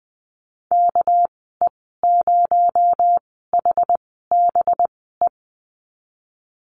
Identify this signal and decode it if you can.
Morse code, heard as a keyed tone, about 20 wpm, spaced as for 15 wpm, 700 Hz. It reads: KE0HBE
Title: 238 Lesson 38g - Letter q - Call Signs 20wpm